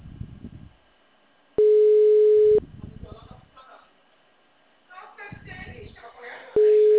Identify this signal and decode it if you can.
Ringback tone (Europe, 425 Hz), with speech.